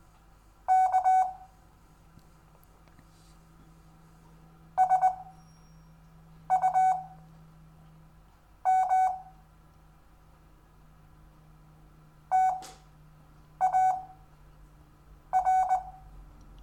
Morse code, heard as a keyed tone, about 20 wpm, 750 Hz. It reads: K S U M T A R
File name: cw / Lesson4 REVIEW